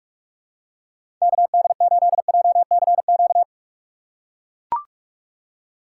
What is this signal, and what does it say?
Morse code, heard as a keyed tone, about 44 wpm, 700 Hz. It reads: KD8JCQ
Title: USA Call Signs - 6 Characters - 6x Character Spacing - 45wpm